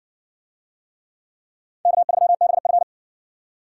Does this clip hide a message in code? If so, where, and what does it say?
Yes: Morse code from 1.9 to 2.8 s, heard as a keyed tone, about 56 wpm, 700 Hz. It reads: K3BF